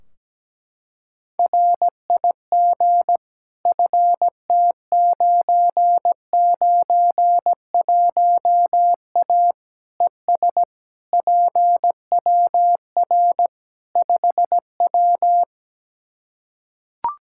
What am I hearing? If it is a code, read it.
Morse code, heard as a keyed tone, about 17 wpm, 700 Hz. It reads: RIG FT991A ES PWR 5W
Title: Practice QSO - Exchange 2 - Info - 17wpm